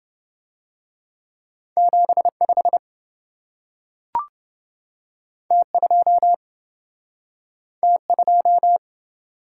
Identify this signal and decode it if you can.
Morse code, heard as a keyed tone, about 28 wpm, 700 Hz. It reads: 75 T2 T2